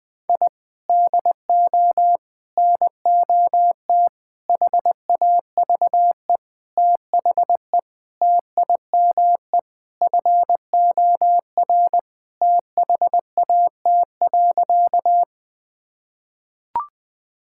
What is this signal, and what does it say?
Morse code, heard as a keyed tone, about 21 wpm, 700 Hz. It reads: I DO NOT HAVE THE TIME FOR THAT.
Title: Sentences with 5-Word Phrases - Spoken Beginning - 20wpm